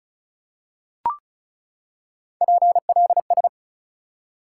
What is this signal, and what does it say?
Morse code, heard as a keyed tone, about 34 wpm, 700 Hz. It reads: PLS
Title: Top 180 QSO Elements 35wpm